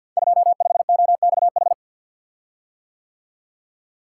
Morse code, heard as a keyed tone, about 50 wpm, 700 Hz. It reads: 25OXH
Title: Code Groups - 5 Letter:Numbers - 50wpm